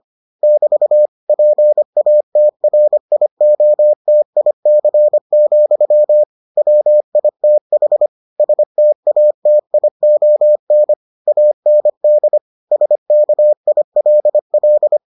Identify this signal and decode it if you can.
Morse code, heard as a keyed tone, about 25 wpm, 600 Hz. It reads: <BT> PATRIOTIC, WITH STATION AND SKILL